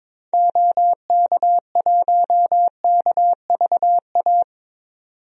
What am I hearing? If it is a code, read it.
Morse code, heard as a keyed tone, about 21 wpm, 700 Hz. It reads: OK1KVA
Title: DXCC-100 Call Signs - 6 Characters - 22wpm